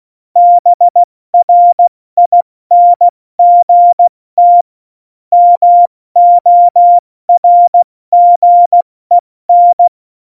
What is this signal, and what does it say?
Morse code, heard as a keyed tone, about 16 wpm, 700 Hz. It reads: BRINGT MORGEN
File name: Morsekurs / Witze / Witz 24. November 2020, 70 F80